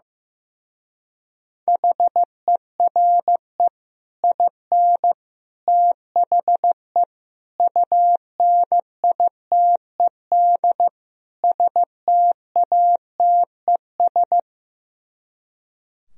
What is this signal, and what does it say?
Morse code, heard as a keyed tone, about 15 wpm, 700 Hz. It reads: HERE IN THE UNITED STATES
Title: Speed-Racing - Sets of 5 Words 15wpm